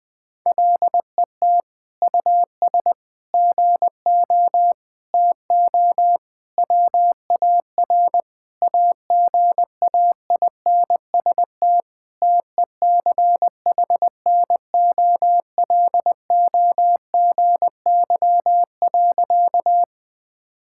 Morse code, heard as a keyed tone, about 20 wpm, 700 Hz. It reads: LET US GO TO WAR AGAINST TECHNOLOGY.
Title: Sentences from Top 1000 Words - Review Part 5 - 20wpm